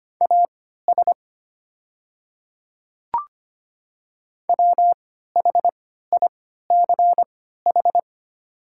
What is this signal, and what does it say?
Morse code, heard as a keyed tone, about 25 wpm, spaced as for 14 wpm, 700 Hz. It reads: AS WHICH
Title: 3x Character Spacing - Top 100 Words - 25wpm